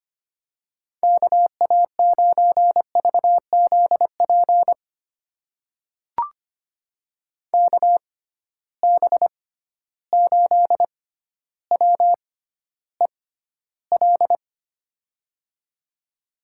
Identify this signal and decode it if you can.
Morse code, heard as a keyed tone, about 25 wpm, spaced as for 13 wpm, 700 Hz. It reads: KA9VZP KB8WEL
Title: USA Call Signs - 6 Characters - 6x Character Spacing - 25wpm